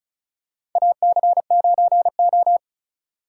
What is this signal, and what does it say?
Morse code, heard as a keyed tone, about 36 wpm, 690 Hz. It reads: AC9O
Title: USA Call Signs - 4 Characters - 3x Character Spacing - 35wpm